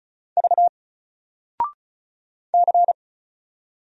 Morse code, heard as a keyed tone, about 36 wpm, 700 Hz. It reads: V C
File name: Single-Letter-Number-35wpm